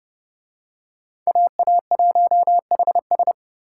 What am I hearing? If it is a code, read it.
Morse code, heard as a keyed tone, about 26 wpm, 700 Hz. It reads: AA1HS